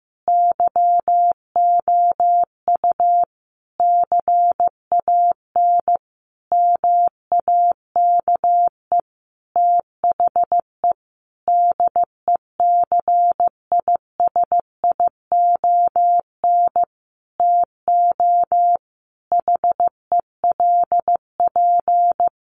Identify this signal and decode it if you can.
Morse code, heard as a keyed tone, about 15 wpm, 700 Hz. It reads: YOU CAN MAKE THE DECISION TO HELP
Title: Sentences from Top 1400 Words 15wpm